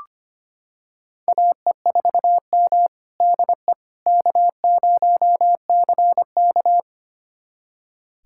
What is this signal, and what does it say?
Morse code, heard as a keyed tone, about 25 wpm, 700 Hz. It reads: AE4M DE K0CK
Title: USA Callsign DE Callsign - 4 Characters - 25wpm